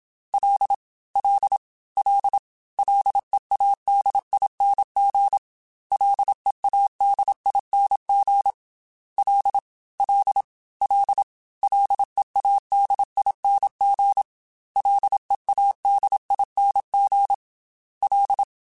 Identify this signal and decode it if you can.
Morse code, heard as a keyed tone, about 27 wpm, 800 Hz. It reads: L L L LEADING LEADING L L L LEADING LEADING L
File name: F and L / F and L mp3 / F and L Prefixes 3x2 Twice @25